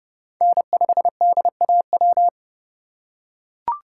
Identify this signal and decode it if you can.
Morse code, heard as a keyed tone, about 30 wpm, 700 Hz. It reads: N5DAW
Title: USA Call Signs - 5 Characters - 2x Character Spacing - 30wpm